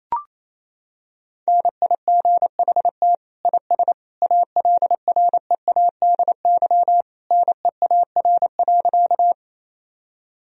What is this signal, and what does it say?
Morse code, heard as a keyed tone, about 28 wpm, 700 Hz. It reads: NIGHT IS ALREADY NEAR.
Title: Sentences from Top 1000 Words - Review Part 1 - 28wpm